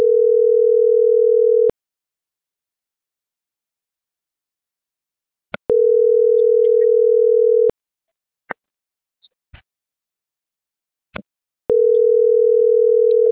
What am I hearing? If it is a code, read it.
Ringback tone (North America), 440 + 480 Hz.